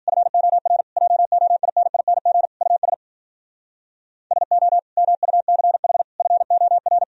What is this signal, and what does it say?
Morse code, heard as a keyed tone, about 51 wpm, 700 Hz. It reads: FOR JOINING US SO MUCH FOR